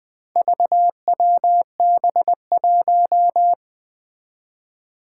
Morse code, heard as a keyed tone, about 21 wpm, 700 Hz. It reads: VWB1
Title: Four Challenging Letter-Number Combinations 20wpm